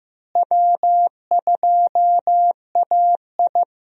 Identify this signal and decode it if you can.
Morse code, heard as a keyed tone, about 15 wpm, 700 Hz. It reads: W2AI
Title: USA Call Signs - 4 Characters - Repeated 3x - 15wpm